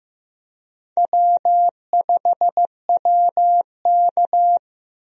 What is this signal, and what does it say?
Morse code, heard as a keyed tone, about 16 wpm, 700 Hz. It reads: W5WK
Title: USA Call Signs - 4 Characters - Repeated 2x - 15wpm